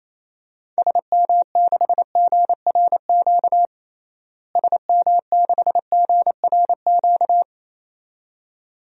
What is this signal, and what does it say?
Morse code, heard as a keyed tone, about 28 wpm, 700 Hz. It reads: SM6GRQ SM6GRQ
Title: DXCC-100 Call Signs - 6 Characters - Repeated 2x - 28wpm